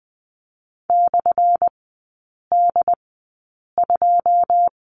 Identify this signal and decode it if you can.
Morse code, heard as a keyed tone, about 20 wpm, spaced as for 8 wpm, 700 Hz. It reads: /D2
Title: 178 Lesson 29e - Slash - Three Characters 20wpm